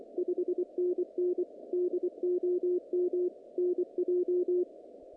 Morse code, heard as a keyed tone, about 24 wpm, 360 Hz, with radio noise. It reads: 5NN DOM NJ
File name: contests / naqp-cw-k5tr-op-k5pi- -2200